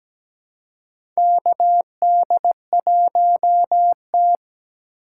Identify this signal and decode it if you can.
Morse code, heard as a keyed tone, about 17 wpm, 700 Hz. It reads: KD1T
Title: USA Call Signs - 4 Characters - 8x Character Spacing - 17wpm